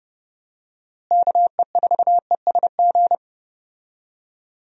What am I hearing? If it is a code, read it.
Morse code, heard as a keyed tone, about 29 wpm, 700 Hz. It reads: KE4ESG
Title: USA Call Signs - 6 Characters - Repeated 3x - 30wpm